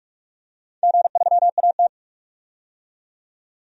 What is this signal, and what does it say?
Morse code, heard as a keyed tone, about 43 wpm, 700 Hz. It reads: G3AT
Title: DXCC-100 Call Signs - 4 Characters - 45wpm